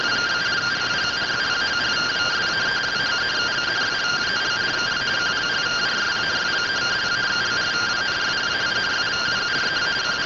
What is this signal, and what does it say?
Two-tone teleprinter signal, about 170 Hz shift.
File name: findings / CORONAL fr69 nb202 3mm.gen-34 rtty rx